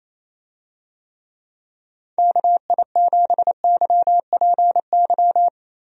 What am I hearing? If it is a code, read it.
Morse code, heard as a keyed tone, about 28 wpm, 700 Hz. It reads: KI7YPY
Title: USA Call Signs - 6 Characters - Repeated 2x - 28wpm